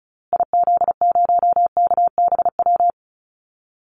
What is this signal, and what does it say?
Morse code, heard as a keyed tone, about 34 wpm, 700 Hz. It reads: IZ0KBW
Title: Call Signs Repeated 2x - 35wpm